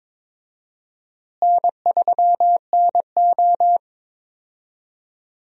Morse code, heard as a keyed tone, about 22 wpm, 700 Hz. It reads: N3NO
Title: USA Call Signs - 4 Characters - Repeated 2x - 22wpm